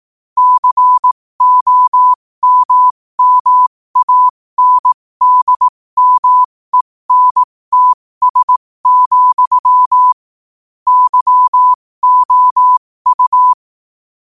Morse code, heard as a keyed tone, about 18 wpm, spaced as for 15 wpm, 1000 Hz. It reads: COMMANDMENTS, YOU
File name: qotdfiles / 15wpm